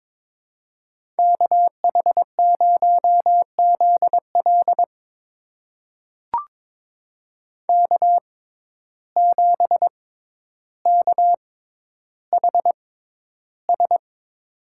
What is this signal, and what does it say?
Morse code, heard as a keyed tone, about 22 wpm, spaced as for 11 wpm, 700 Hz. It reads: KH0ZL K7KHS
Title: USA Call Signs - 5 Characters - 6x Character Spacing - 22wpm